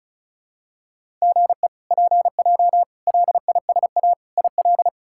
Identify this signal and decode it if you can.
Morse code, heard as a keyed tone, about 36 wpm, 700 Hz. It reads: GE PJ LISA IL
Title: K1USN SST QSO Practice - 35wpm